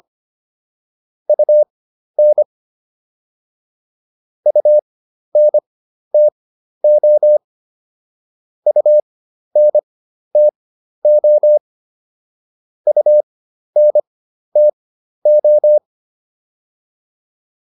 Morse code, heard as a keyed tone, about 25 wpm, spaced as for 12 wpm, 600 Hz. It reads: UN UNTO UNTO UNTO